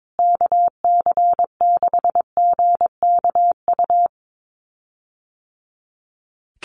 Morse code, heard as a keyed tone, about 22 wpm, 700 Hz. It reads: KC6GKU
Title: USA Call Signs Repeated 3x 22wpm